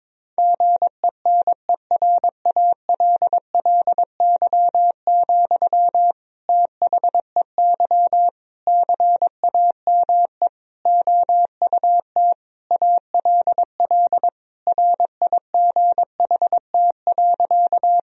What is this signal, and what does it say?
Morse code, heard as a keyed tone, about 21 wpm, 700 Hz. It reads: GENERALLY, THEY CAME OUT ALL RIGHT.
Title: Sentences from Top 1000 Words - Review Part 4 - 22wpm